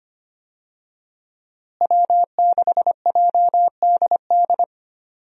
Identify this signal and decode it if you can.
Morse code, heard as a keyed tone, about 25 wpm, 700 Hz. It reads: W6JDD
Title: USA Call Signs - 5 Characters - Repeated 2x - 25wpm